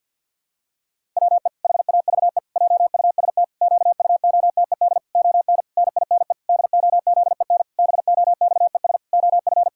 Morse code, heard as a keyed tone, about 50 wpm, 700 Hz. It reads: WE HAVE JUST QUOTED ON NINE DOZEN BOXES OF
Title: Pangram Challenge v2 - 50wpm